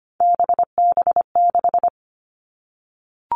Morse code, heard as a keyed tone, about 25 wpm, 700 Hz. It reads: BB6
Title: All Cut Numbers - 3-Digit 25wpm